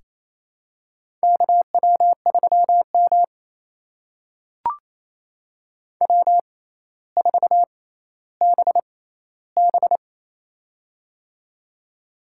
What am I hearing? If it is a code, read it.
Morse code, heard as a keyed tone, about 28 wpm, spaced as for 14 wpm, 700 Hz. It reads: KW3M W4BB
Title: USA Call Signs - 4 Characters - 6x Character Spacing - 28wpm